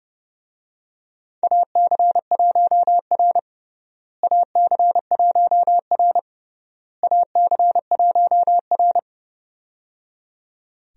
Morse code, heard as a keyed tone, about 30 wpm, 700 Hz. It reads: AC1R AC1R AC1R
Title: USA Call Signs - 4 Characters - Repeated 3x - 30wpm